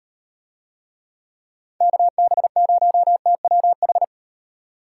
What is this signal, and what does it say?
Morse code, heard as a keyed tone, about 38 wpm, 700 Hz. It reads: KB0TWH